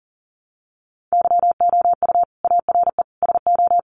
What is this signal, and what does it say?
Morse code, heard as a keyed tone, about 40 wpm, 700 Hz. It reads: YOU ARE SO
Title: Sentences from Top 1200 Words 40wpm